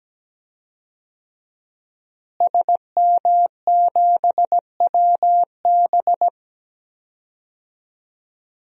Morse code, heard as a keyed tone, about 17 wpm, 700 Hz. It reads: SM7WB
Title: DXCC-100 Call Signs - 5 Characters - Repeated 2x - 17wpm